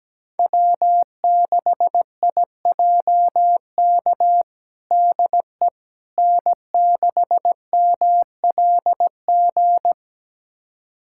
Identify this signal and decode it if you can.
Morse code, heard as a keyed tone, about 17 wpm, 700 Hz. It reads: W6IJK DE N6MLG
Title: USA Callsign DE Callsign - 5 Characters - 17wpm